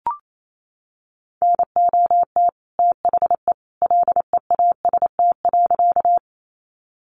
Morse code, heard as a keyed tone, about 28 wpm, 700 Hz. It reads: NOT THE LEAST.
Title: Sentences from Top 700 Words 28wpm